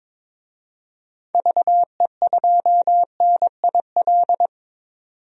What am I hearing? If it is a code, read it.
Morse code, heard as a keyed tone, about 21 wpm, 700 Hz. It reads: VE2NIL
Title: DXCC-100 Call Signs - 6 Characters - 22wpm